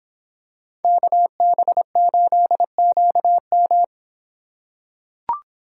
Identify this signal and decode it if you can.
Morse code, heard as a keyed tone, about 26 wpm, 700 Hz. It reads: KB8QM